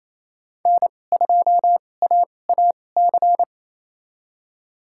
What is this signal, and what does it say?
Morse code, heard as a keyed tone, about 28 wpm, spaced as for 20 wpm, 700 Hz. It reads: N2AAC
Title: USA Call Signs - 5 Characters - 2x Character Spacing - 28wpm